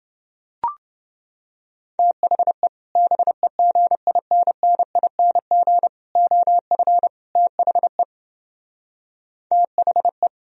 Morse code, heard as a keyed tone, about 28 wpm, 700 Hz. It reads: THE BEGINNING OF THE THE